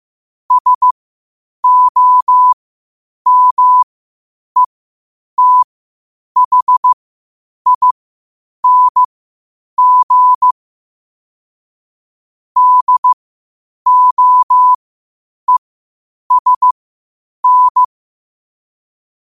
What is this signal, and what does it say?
Morse code, heard as a keyed tone, about 15 wpm, spaced as for 8 wpm, 1000 Hz. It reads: SOMETHING DOESN
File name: qotdfiles / 8wpm